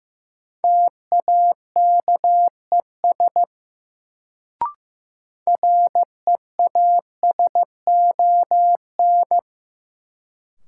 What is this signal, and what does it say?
Morse code, heard as a keyed tone, about 15 wpm, 700 Hz. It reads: TAKES REASON
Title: Sets of 5 Words Encore - New Word Drill - 15wpm